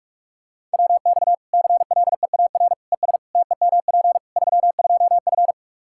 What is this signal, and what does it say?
Morse code, heard as a keyed tone, about 46 wpm, 700 Hz. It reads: WX CLEAR ES TEMP 32F
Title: Practice QSO - Exchange 2 - Info - 45wpm